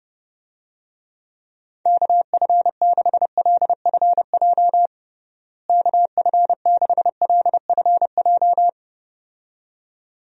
Morse code, heard as a keyed tone, about 32 wpm, 700 Hz. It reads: KF6LFJ KF6LFJ
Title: USA Call Signs - 6 Characters - Repeated 2x - 30wpm